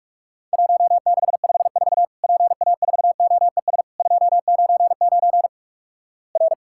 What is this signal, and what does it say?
Morse code, heard as a keyed tone, about 48 wpm, 700 Hz. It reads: 1654 PA4OES 299 R
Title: Worked All Europe DX Contest - QTC Exchange - 45wpm